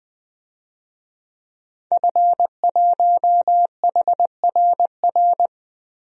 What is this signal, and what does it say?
Morse code, heard as a keyed tone, about 21 wpm, 700 Hz. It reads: F1HRR
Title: DXCC-100 Call Signs - 5 Characters - Repeated 3x - 20wpm